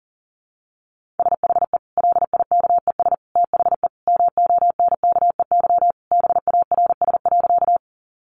Morse code, heard as a keyed tone, about 40 wpm, 700 Hz. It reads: SHE LIKES THE MONKEY BARS.
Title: Easy Phrases 40wpm